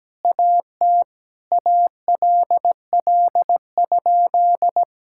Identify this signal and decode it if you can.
Morse code, heard as a keyed tone, about 17 wpm, 700 Hz. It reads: AT ALL?
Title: Sentences from Top 1600 Words 17wpm